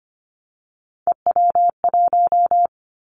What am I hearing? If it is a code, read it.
Morse code, heard as a keyed tone, about 25 wpm, 700 Hz. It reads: EW1
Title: Callsign Prefixes for Top 50 DXCC - 25wpm